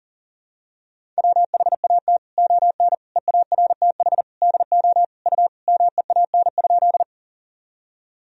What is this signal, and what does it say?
Morse code, heard as a keyed tone, about 40 wpm, 700 Hz. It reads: WHAT ON EARTH DO U MEAN?
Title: Sentences from Top 200 Words and Abbreviations 40wpm